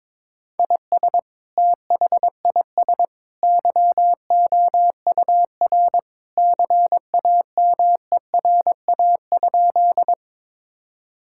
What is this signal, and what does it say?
Morse code, heard as a keyed tone, about 21 wpm, 700 Hz. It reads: IS THIS YOUR CAMERA?
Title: Sentences from Top 1000 Words - Review Part 5 - 22wpm